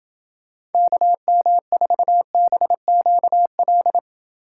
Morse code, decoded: KM4BQL

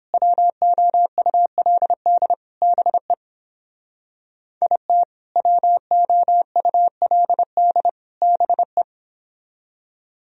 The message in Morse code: WOULD BE IT WOULD BE

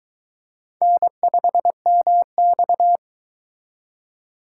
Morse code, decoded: N5MX

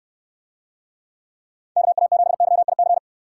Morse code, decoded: KA7YIC

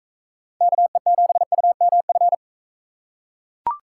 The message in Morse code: KE7UMF

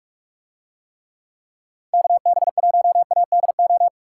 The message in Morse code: KB1ADO

